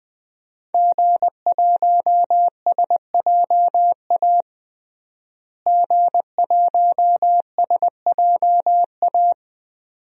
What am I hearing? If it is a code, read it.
Morse code, heard as a keyed tone, about 20 wpm, 700 Hz. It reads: G1SJA G1SJA
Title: DXCC-100 Call Signs - 5 Characters - Repeated 2x - 20wpm